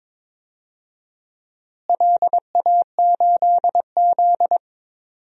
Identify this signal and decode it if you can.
Morse code, heard as a keyed tone, about 22 wpm, 700 Hz. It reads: LA8Z
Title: DXCC-100 Call Signs - 4 Characters - Repeated 2x - 22wpm